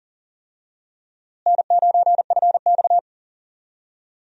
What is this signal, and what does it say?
Morse code, heard as a keyed tone, about 40 wpm, 700 Hz. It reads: N9FX
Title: USA Call Signs - 4 Characters - Repeated 2x - 40wpm